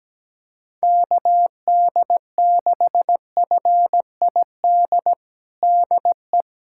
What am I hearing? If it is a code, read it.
Morse code, heard as a keyed tone, about 17 wpm, 700 Hz. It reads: KD6FID DE
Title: USA Callsign DE Callsign - 6 Characters - 17wpm